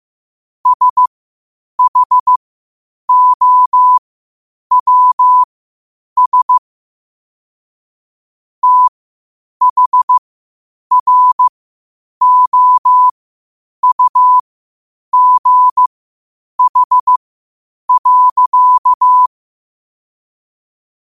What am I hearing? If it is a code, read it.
Morse code, heard as a keyed tone, about 15 wpm, spaced as for 8 wpm, 1000 Hz. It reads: SHOWS THROUGH.
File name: qotdfiles / 8wpm